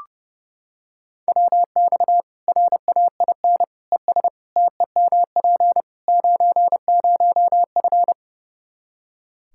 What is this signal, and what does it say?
Morse code, heard as a keyed tone, about 32 wpm, 700 Hz. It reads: WX RAIN ES TEMP 90F
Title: Practice QSO - Exchange 2 - Info - 30wpm